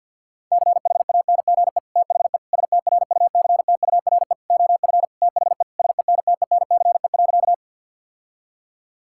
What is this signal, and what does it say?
Morse code, heard as a keyed tone, about 50 wpm, 700 Hz. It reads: CHANGE THE STRUCTURE OF THE SENTENCE.
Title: Sentences from Top 1000 Words - Review Part 6 - 50wpm